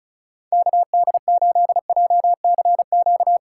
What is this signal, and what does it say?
Morse code, heard as a keyed tone, about 36 wpm, 690 Hz. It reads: KD8JCQ
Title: USA Call Signs - 6 Characters - 2x Character Spacing - 35wpm